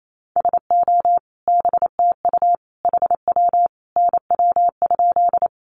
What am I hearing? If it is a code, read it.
Morse code, heard as a keyed tone, about 28 wpm, 700 Hz. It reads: SO BTU HW NW?
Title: Common QSO Phrases 28wpm